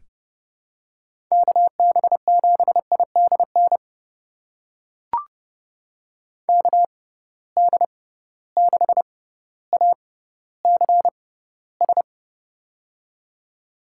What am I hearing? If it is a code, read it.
Morse code, heard as a keyed tone, about 30 wpm, spaced as for 15 wpm, 700 Hz. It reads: KB7IDN KD6ACS
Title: USA Call Signs - 6x Character Spacing - 30wpm